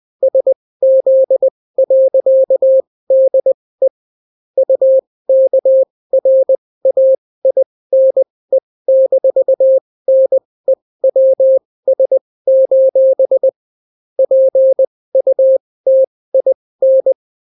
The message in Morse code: SZ.DE UKRAINE-NEWS: PUTIN